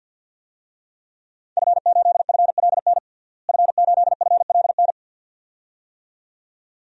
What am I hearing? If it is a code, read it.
Morse code, heard as a keyed tone, about 49 wpm, 700 Hz. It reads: F8FLN F8FLN
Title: DXCC-100 Call Signs - 5 Characters - Repeated 2x - 50wpm